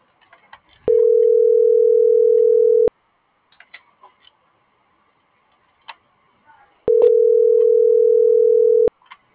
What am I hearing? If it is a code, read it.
Ringback tone (North America), 440 + 480 Hz.